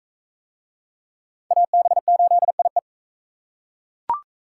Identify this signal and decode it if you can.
Morse code, heard as a keyed tone, about 39 wpm, 700 Hz. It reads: AB8IE